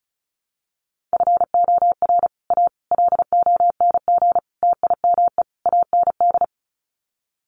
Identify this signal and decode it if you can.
Morse code, heard as a keyed tone, about 34 wpm, 700 Hz. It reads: FOR A LONG TIME AND
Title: Sets of 5 Words Encore2 35wpm